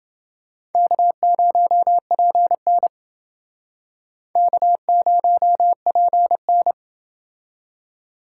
Morse code, heard as a keyed tone, about 28 wpm, 700 Hz. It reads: K0PN K0PN